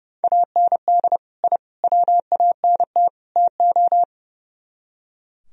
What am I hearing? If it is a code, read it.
Morse code, heard as a keyed tone, about 38 wpm, 700 Hz. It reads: AND I WANT TO